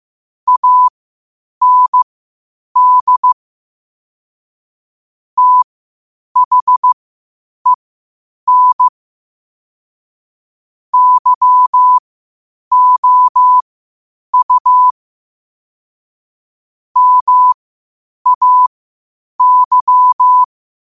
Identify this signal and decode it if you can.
Morse code, heard as a keyed tone, about 15 wpm, spaced as for 8 wpm, 1000 Hz. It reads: AND THEN YOU MAY
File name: qotdfiles / 8wpm